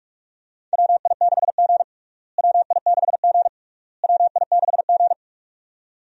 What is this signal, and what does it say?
Morse code, heard as a keyed tone, about 46 wpm, 700 Hz. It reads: WI6G WI6G WI6G
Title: USA Call Signs - 4 Characters - Repeated 3x - 45wpm